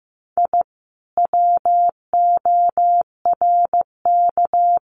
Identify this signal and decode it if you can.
Morse code, heard as a keyed tone, about 15 wpm, 700 Hz. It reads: I WORK
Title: Common QSO Phrases Encore 15wpm